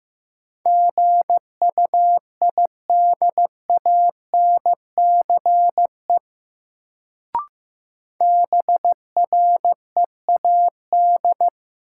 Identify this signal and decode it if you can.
Morse code, heard as a keyed tone, about 15 wpm, 700 Hz. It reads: GUIDANCE BREAD
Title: Sentences from Top 1600 Words - New Word Drill - 15wpm